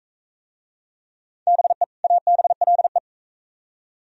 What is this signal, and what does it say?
Morse code, heard as a keyed tone, about 41 wpm, 700 Hz. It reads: BE ABLE